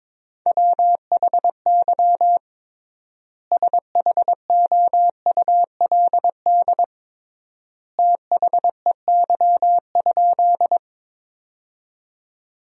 Morse code, heard as a keyed tone, about 22 wpm, 700 Hz. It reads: WHY SHOULD THEY?
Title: 3x Word-Spacing - Sentences from Top 300 Words - 22wpm